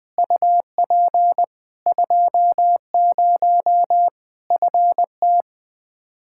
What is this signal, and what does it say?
Morse code, heard as a keyed tone, about 20 wpm, 700 Hz. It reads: UP 20 FT
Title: Practice QSO - Exchange 2 - Info - 20wpm